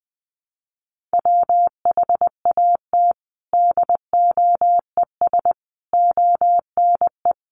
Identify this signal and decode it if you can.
Morse code, heard as a keyed tone, about 20 wpm, 700 Hz. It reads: WHAT DOES ONE
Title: Sentences from Top 200 Words 20wpm